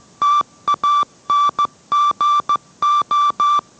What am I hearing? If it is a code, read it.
Morse code, heard as a keyed tone, about 15 wpm, 1170 Hz. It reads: TANGO